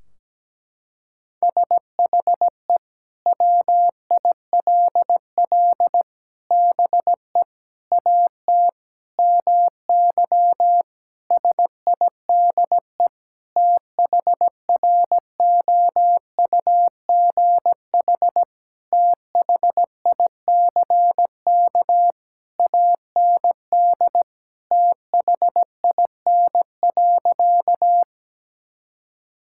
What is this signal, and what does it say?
Morse code, heard as a keyed tone, about 17 wpm, 700 Hz. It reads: SHE WILL BE AT MY SIDE THROUGH THICK AND THIN.
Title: Sentences from Top 1600 Words 17wpm